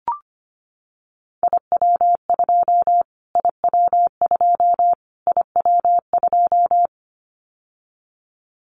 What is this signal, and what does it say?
Morse code, heard as a keyed tone, about 25 wpm, 700 Hz. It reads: IW2 IW2 IW2
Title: Callsign Prefixes for Top 25 DXCC - Repeated 3x - 25wpm